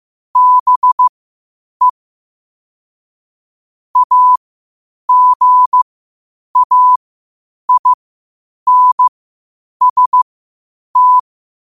Morse code, heard as a keyed tone, about 15 wpm, spaced as for 8 wpm, 1000 Hz. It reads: BE AGAINST
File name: qotdfiles / 8wpm